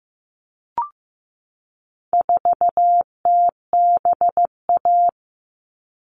Morse code, heard as a keyed tone, about 15 wpm, 700 Hz. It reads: 4TBA
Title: All Cut Numbers - 4-Digit 15wpm